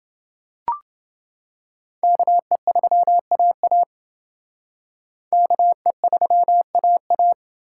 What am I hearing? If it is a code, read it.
Morse code, heard as a keyed tone, about 28 wpm, 700 Hz. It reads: KE3AA KE3AA